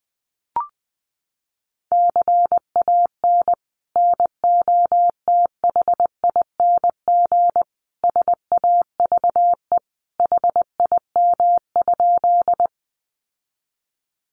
Morse code, heard as a keyed tone, about 20 wpm, 700 Hz. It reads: CAN NOTHING SAVE HIM?
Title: Sentences from Top 1100 Words 20wpm